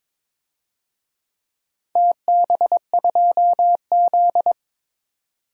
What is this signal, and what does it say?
Morse code, heard as a keyed tone, about 22 wpm, 700 Hz. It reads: TB2Z
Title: DXCC-100 Call Signs - 4 Characters - 22wpm